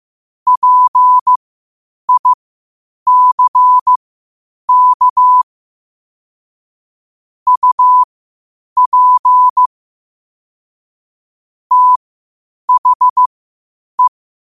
Morse code, heard as a keyed tone, about 15 wpm, spaced as for 8 wpm, 1000 Hz. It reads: PICK UP THE
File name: qotdfiles / 8wpm